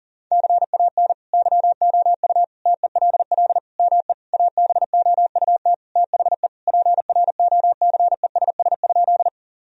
Morse code, heard as a keyed tone, about 40 wpm, 700 Hz. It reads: CAN YOU TELL ME ABOUT THE PROCESS?
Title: Sentences from Top 1000 Words - Review Part 3 - 40wpm